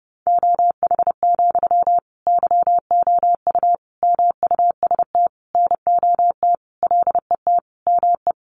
Morse code, heard as a keyed tone, about 30 wpm, 700 Hz. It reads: OH, YOU MUST NOT LET ME
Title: Sentences from Top 1100 Words 30wpm